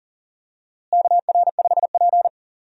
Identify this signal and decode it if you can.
Morse code, heard as a keyed tone, about 40 wpm, 700 Hz. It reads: KR5P
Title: USA Call Signs - 4 Characters - Repeated 3x - 40wpm